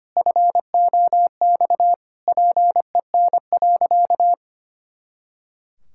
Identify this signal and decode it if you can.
Morse code, heard as a keyed tone, about 25 wpm, 700 Hz. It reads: FOX PEN.
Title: Pangram Challenge v2 - 25wpm